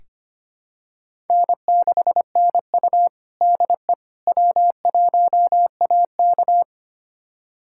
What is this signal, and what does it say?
Morse code, heard as a keyed tone, about 25 wpm, 700 Hz. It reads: N6NU DE W1AK
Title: USA Callsign DE Callsign - 4 Characters - 25wpm